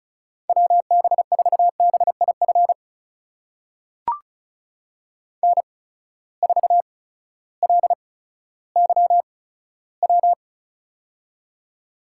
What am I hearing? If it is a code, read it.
Morse code, heard as a keyed tone, about 36 wpm, spaced as for 16 wpm, 700 Hz. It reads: WB4BIF N4LYW
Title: USA Call Signs - 8x Character Spacing - 35wpm